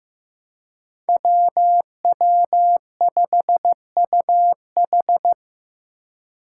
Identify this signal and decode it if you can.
Morse code, heard as a keyed tone, about 15 wpm, 700 Hz. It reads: WW5UH
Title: Five Challenging Letter-Number Combinations 15wpm